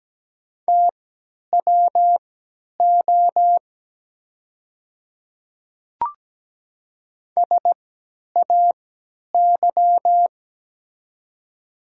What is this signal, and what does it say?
Morse code, heard as a keyed tone, about 17 wpm, spaced as for 10 wpm, 700 Hz. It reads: TWO SAY
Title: 3x Character Spacing - Top 200 Words - 17wpm